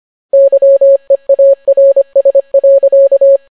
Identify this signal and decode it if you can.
Morse code, heard as a keyed tone, about 25 wpm, 550 Hz. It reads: YEARS.